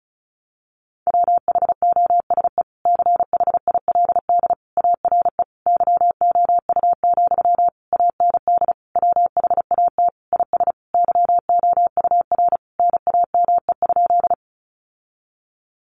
Morse code, heard as a keyed tone, about 34 wpm, 700 Hz. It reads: WHOSE CHILD ARE YOU, AND WHAT IS YOUR NAME?
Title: Sentences from Top 1200 Words 35wpm